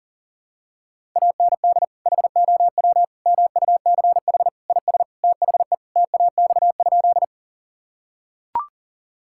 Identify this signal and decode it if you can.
Morse code, heard as a keyed tone, about 40 wpm, 700 Hz. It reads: AND HOW MUCH IS THE TAX?
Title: Sentences from Top 1000 Words - Review Part 5 - 40wpm